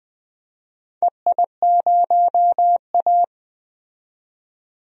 Morse code, heard as a keyed tone, about 20 wpm, 700 Hz. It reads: EI0A
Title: DXCC-100 Call Signs - 4 Characters - 20wpm